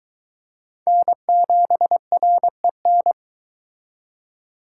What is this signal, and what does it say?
Morse code, heard as a keyed tone, about 24 wpm, 700 Hz. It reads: N7REN